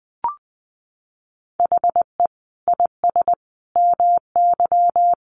Morse code, heard as a keyed tone, about 20 wpm, 700 Hz. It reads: HE IS MY
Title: Sentences from Top 300 Words 20wpm